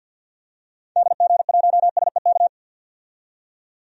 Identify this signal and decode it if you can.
Morse code, heard as a keyed tone, about 50 wpm, 700 Hz. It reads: DG1SEK
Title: DXCC-100 Call Signs - 6 Characters - 50wpm